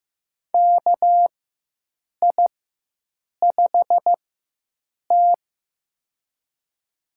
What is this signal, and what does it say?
Morse code, heard as a keyed tone, about 15 wpm, spaced as for 7 wpm, 700 Hz. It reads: KI5T
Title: USA Call Signs - 4 Characters - 4x Character Spacing - 15wpm